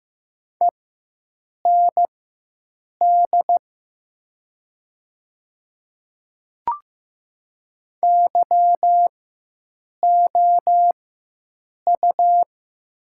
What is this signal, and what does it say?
Morse code, heard as a keyed tone, about 15 wpm, spaced as for 7 wpm, 700 Hz. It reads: END YOU
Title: 4x Character Spacing - Top 300 Words - 15wpm